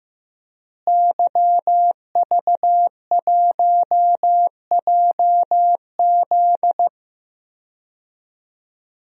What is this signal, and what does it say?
Morse code, heard as a keyed tone, about 15 wpm, 700 Hz. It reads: YV1JZ
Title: DXCC-100 Call Signs - 5 Characters - Repeated 3x - 15wpm